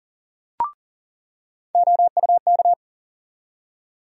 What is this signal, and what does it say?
Morse code, heard as a keyed tone, about 40 wpm, 700 Hz. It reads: OUK
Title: USA Call Sign Postfixes - 3 Letters 40wpm